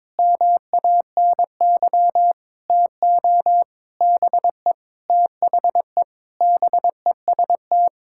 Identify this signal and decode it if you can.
Morse code, heard as a keyed tone, about 22 wpm, 700 Hz. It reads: MANY TO BE THE BEST
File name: Sentences with 7-Word Phrases - 22wpm